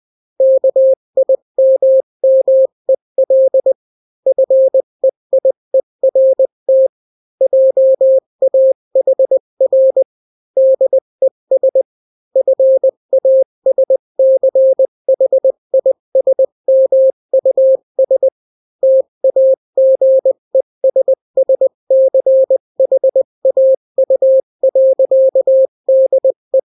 Morse code, heard as a keyed tone, about 20 wpm, 530 Hz. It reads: KIMMEL FEIERT JAHR DES FASCHISMUS TAGESSCHAU.DE
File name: daily / 18wpm